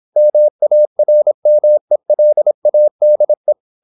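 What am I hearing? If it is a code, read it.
Morse code, heard as a keyed tone, about 26 wpm, 600 Hz. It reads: MARMELADE
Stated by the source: High-speed (QRQ) Morse at 26 wpm.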